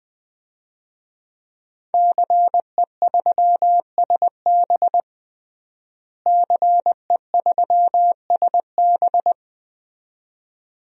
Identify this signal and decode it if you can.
Morse code, heard as a keyed tone, about 20 wpm, 700 Hz. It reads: CE3SB CE3SB
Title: DXCC-100 Call Signs - 5 Characters - Repeated 2x - 20wpm